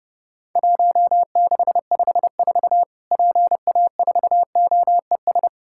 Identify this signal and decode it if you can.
Morse code, heard as a keyed tone, about 30 wpm, 700 Hz. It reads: 1654 PA4OES
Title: Worked All Europe DX Contest - QTC Exchange - 30wpm